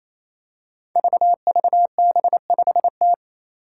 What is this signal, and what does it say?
Morse code, heard as a keyed tone, about 28 wpm, 700 Hz. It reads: VVB5T
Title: Five Challenging Letter-Number Combinations 28wpm